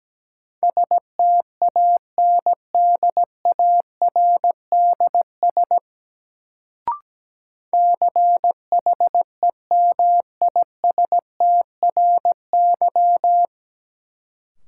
Morse code, heard as a keyed tone, about 17 wpm, 700 Hz. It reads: STANDARDS CHEMISTRY
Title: Sentences from Top 1000 Words - Review Part 6 - New Word Drill - 17wpm